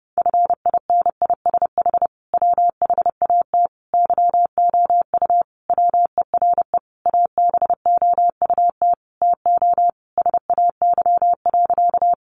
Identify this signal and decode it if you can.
Morse code, heard as a keyed tone, about 30 wpm, 700 Hz. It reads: FINISH WHAT YOU WERE ABOUT TO SAY.
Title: Sentences from Top 1200 Words 30wpm